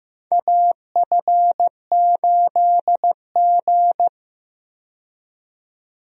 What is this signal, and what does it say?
Morse code, heard as a keyed tone, about 15 wpm, 700 Hz. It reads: AF8G
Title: USA Call Signs - 4 Characters - 15wpm